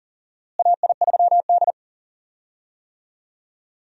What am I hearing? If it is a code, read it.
Morse code, heard as a keyed tone, about 40 wpm, 700 Hz. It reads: AI3D
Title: USA Call Signs - 4 Characters - Repeated 3x - 40wpm